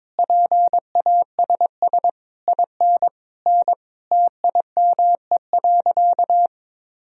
Morse code, heard as a keyed tone, about 22 wpm, 700 Hz. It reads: PASS IN N TIME.
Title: Sentences from Top 400 Words and Abbreviations 22wpm